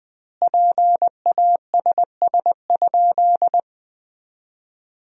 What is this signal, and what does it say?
Morse code, heard as a keyed tone, about 21 wpm, 700 Hz. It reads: PASS?
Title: 2x Word-Spacing - Sentences from Top 400 Words - 20wpm